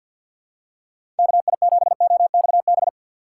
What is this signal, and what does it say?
Morse code, heard as a keyed tone, about 53 wpm, 700 Hz. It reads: KI7OXB
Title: USA Call Signs - 6 Characters - 6x Character Spacing - 50wpm